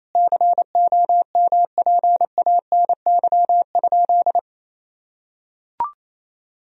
Morse code, heard as a keyed tone, about 28 wpm, 700 Hz. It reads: COMPANY?
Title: Sentences from Top 1000 Words - Review Part 3 - 28wpm